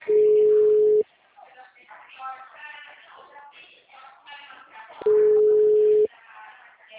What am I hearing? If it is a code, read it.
Ringback tone (Europe, 425 Hz), with speech.